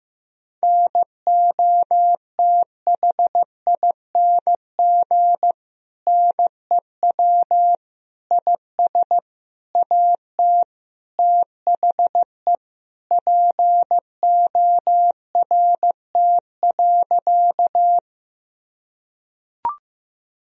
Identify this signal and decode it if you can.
Morse code, heard as a keyed tone, about 16 wpm, 700 Hz. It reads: NOTHING NEW IS AT THE PORT.
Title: Sentences from Top 1000 Words - Review Part 1 - 15wpm